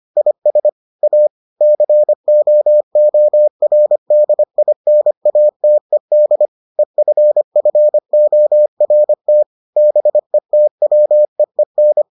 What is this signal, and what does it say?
Morse code, heard as a keyed tone, about 25 wpm, 600 Hz. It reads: IS A COORDINATED EFFORT BETWEEN